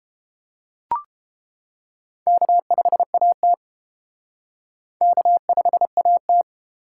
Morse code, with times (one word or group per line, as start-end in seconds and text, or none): 2.3-3.5 K5AT
5.0-6.4 K5AT